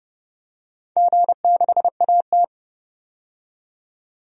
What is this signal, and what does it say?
Morse code, heard as a keyed tone, about 29 wpm, 700 Hz. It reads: G6AT
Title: DXCC-100 Call Signs - 4 Characters - 30wpm